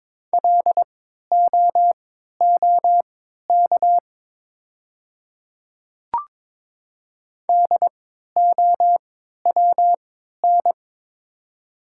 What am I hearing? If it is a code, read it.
Morse code, heard as a keyed tone, about 22 wpm, spaced as for 12 wpm, 700 Hz. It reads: LOOK DOWN
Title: 3x Character Spacing - Top 100 Words - 22wpm